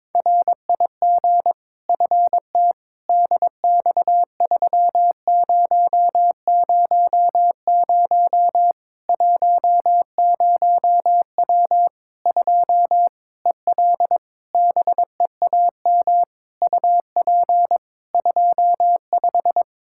Morse code, heard as a keyed tone, about 22 wpm, 700 Hz. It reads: RIG FT DX3000 10W 2 EL BEAM UP 25
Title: Practice QSO - Exchange 2 - Info - 22wpm